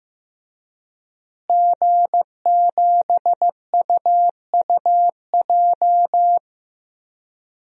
Morse code, heard as a keyed tone, about 15 wpm, 700 Hz. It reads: G7UUJ
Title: DXCC-100 Call Signs - 5 Characters - Repeated 2x - 15wpm